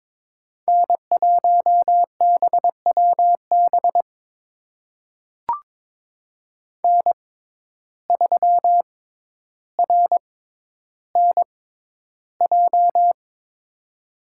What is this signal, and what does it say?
Morse code, heard as a keyed tone, about 22 wpm, spaced as for 11 wpm, 700 Hz. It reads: N1BWB N3RNJ
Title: USA Call Signs - 5 Characters - 6x Character Spacing - 22wpm